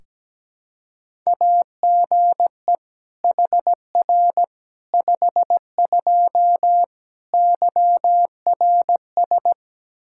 Morse code, heard as a keyed tone, about 17 wpm, 700 Hz. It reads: AGE HR 52 YRS
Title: Practice QSO - Exchange 3 - More Info - 17wpm